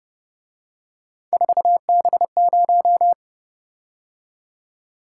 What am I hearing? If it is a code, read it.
Morse code, heard as a keyed tone, about 29 wpm, 700 Hz. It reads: 4B0